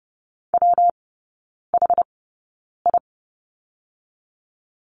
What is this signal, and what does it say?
Morse code, heard as a keyed tone, about 30 wpm, spaced as for 9 wpm, 700 Hz. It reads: WHI
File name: Lesson 19e - Letter w - Three Characters 30wpm